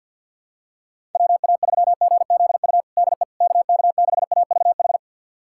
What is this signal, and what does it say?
Morse code, heard as a keyed tone, about 52 wpm, 700 Hz. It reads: WA3GZU DE KK6AVH
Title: USA Callsign DE Callsign - 6 Characters - 50wpm